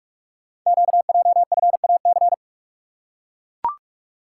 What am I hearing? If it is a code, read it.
Morse code, heard as a keyed tone, about 44 wpm, 690 Hz. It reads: QJFAC
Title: Code Groups - 5 Letters - 45wpm